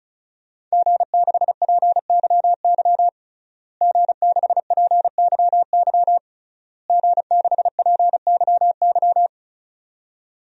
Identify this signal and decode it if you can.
Morse code, heard as a keyed tone, about 36 wpm, 700 Hz. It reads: G6PYY G6PYY G6PYY
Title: DXCC-100 Call Signs - 5 Characters - Repeated 3x - 35wpm